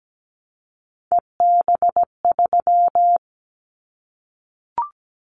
Morse code, heard as a keyed tone, about 17 wpm, 700 Hz. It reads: EB3
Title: Callsign Prefixes for Top 25 DXCC - Repeated 3x - 17wpm